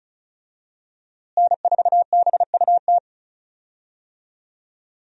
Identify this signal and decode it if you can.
Morse code, heard as a keyed tone, about 36 wpm, 700 Hz. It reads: N4BUT
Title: USA Call Signs - 5 Characters - Repeated 3x - 35wpm